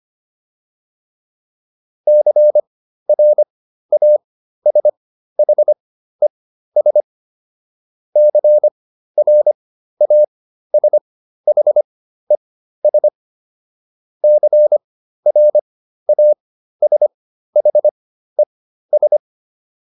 Morse code, heard as a keyed tone, about 25 wpm, spaced as for 13 wpm, 600 Hz. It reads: CRASHES CRASHES CRASHES